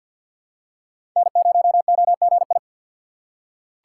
Morse code, heard as a keyed tone, about 47 wpm, 700 Hz. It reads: N0OGI
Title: USA Call Signs - 5 Characters - 2x Character Spacing - 50wpm